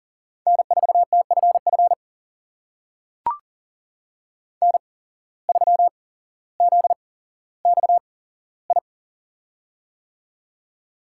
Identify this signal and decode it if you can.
Morse code, heard as a keyed tone, about 40 wpm, spaced as for 17 wpm, 700 Hz. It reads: N4TFF N3ZXI
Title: USA Call Signs - 5 Characters - 8x Character Spacing - 40wpm